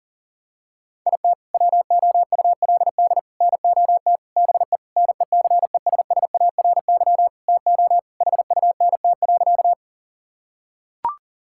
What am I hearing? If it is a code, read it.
Morse code, heard as a keyed tone, about 40 wpm, 700 Hz. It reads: IT WOULD NOT BE NECESSARY TO HUNT.
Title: Sentences from Top 1600 Words 40wpm